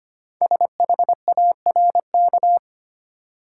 Morse code, heard as a keyed tone, about 25 wpm, 700 Hz. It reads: SHARK
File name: practice / 25wpm / Easy Words with Hints - 25wpm